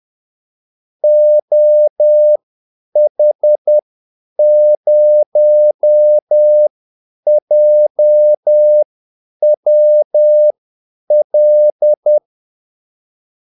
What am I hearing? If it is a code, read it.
Morse code, heard as a keyed tone, about 10 wpm, spaced as for 8 wpm, 600 Hz. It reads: OH0JWL
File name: Callsigns-10-8wpm